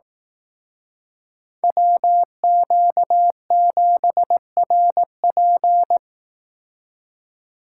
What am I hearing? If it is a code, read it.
Morse code, heard as a keyed tone, about 18 wpm, 700 Hz. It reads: WQ7RP
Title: Speed Racing - USA 2x2 Call Signs 15wpm